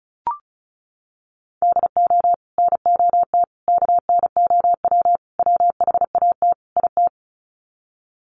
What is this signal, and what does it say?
Morse code, heard as a keyed tone, about 34 wpm, 700 Hz. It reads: DO NOT KNOW WHAT IT
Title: Sets of 5 Words 35wpm